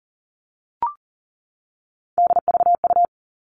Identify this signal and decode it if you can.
Morse code, heard as a keyed tone, about 40 wpm, 700 Hz. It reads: DVU
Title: All Cut Numbers - 3-Digit 40wpm